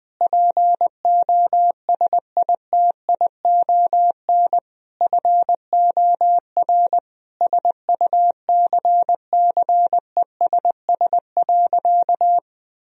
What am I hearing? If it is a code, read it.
Morse code, heard as a keyed tone, about 20 wpm, 700 Hz. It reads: POSITION FOR SUCCESS.
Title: Sentences from Top 1000 Words - Review Part 4 - 20wpm